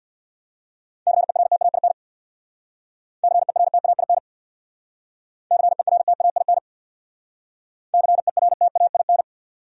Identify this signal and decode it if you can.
Morse code, heard as a keyed tone, about 62 wpm, 700 Hz. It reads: CERTAIN CERTAIN CERTAIN CERTAIN
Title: Speed-Racing Top 500 Words 50wpm